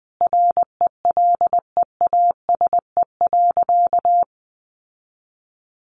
Morse code, heard as a keyed tone, about 20 wpm, 700 Hz. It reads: RELEASE.
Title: Sentences from Top 1100 Words 20wpm